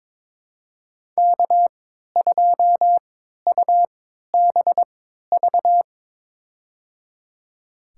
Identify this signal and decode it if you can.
Morse code, heard as a keyed tone, about 22 wpm, spaced as for 13 wpm, 700 Hz. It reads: K2UBV
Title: USA Call Signs - 5 Characters - 3x Character Spacing - 22wpm